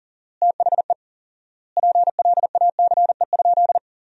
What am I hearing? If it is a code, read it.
Morse code, heard as a keyed tone, about 40 wpm, 700 Hz. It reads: THE PLACE?
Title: 4x Word-Spacing - Sentences from Top 200 Words - 40wpm